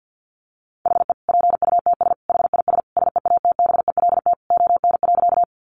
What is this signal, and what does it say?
Morse code, heard as a keyed tone, about 50 wpm, 700 Hz. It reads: HE PUTS HIS SEATBELT ON.
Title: Additional Easy Phrases 50wpm